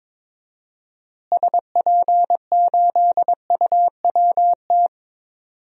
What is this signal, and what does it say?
Morse code, heard as a keyed tone, about 22 wpm, 700 Hz. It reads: SP8UWT
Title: DXCC-100 Call Signs - 6 Characters - Repeated 2x - 22wpm